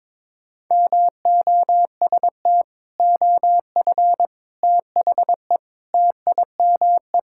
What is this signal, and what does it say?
Morse code, heard as a keyed tone, about 22 wpm, 700 Hz. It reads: MOST OF THE TIME
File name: practice / 22wpm / Sentences with 4-Word Phrases - 22wpm